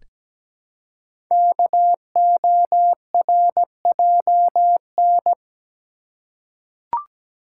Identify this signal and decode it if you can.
Morse code, heard as a keyed tone, about 17 wpm, 700 Hz. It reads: KORJN
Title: Code Groups - 5 Letters - 17wpm